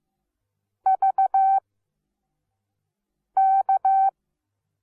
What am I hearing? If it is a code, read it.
Morse code, heard as a keyed tone, about 15 wpm, spaced as for 4 wpm, 770 Hz. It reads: VK
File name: Onlinelehrgang / morsen / DJ4UF Morsen17